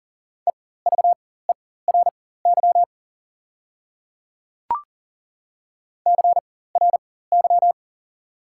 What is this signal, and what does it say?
Morse code, heard as a keyed tone, about 39 wpm, spaced as for 19 wpm, 690 Hz. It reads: EVERY CRY
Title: 4x Character Spacing - Top 500 Words - 40wpm